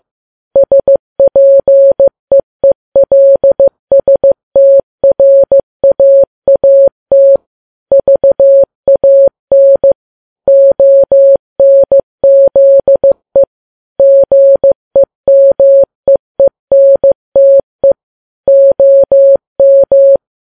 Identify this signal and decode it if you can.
Morse code, heard as a keyed tone, about 15 wpm, 550 Hz. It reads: SPEELSTRAAT VAN ONZE GEMEENTE OM
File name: Files Web tls / Morse on9fv / morse15kps